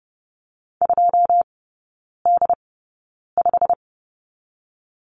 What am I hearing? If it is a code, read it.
Morse code, heard as a keyed tone, about 30 wpm, spaced as for 9 wpm, 700 Hz. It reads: 2D5
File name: Lesson 16e - Letter c - Three Characters 30wpm